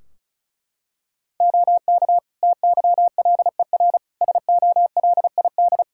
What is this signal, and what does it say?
Morse code, heard as a keyed tone, about 34 wpm, 700 Hz. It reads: OK TYLER SOLID
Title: Practice QSO - Exchange 3 - More Info - 35wpm